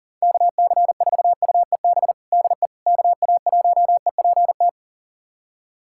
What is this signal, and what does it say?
Morse code, heard as a keyed tone, about 40 wpm, 700 Hz. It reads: KC4UEB DE KA1EPT
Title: USA Callsign DE Callsign - 6 Characters - 40wpm